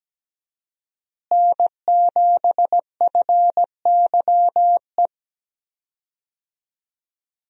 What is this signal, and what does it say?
Morse code, heard as a keyed tone, about 17 wpm, 700 Hz. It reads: N7FYE
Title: USA Call Signs - 5 Characters - Repeated 2x - 17wpm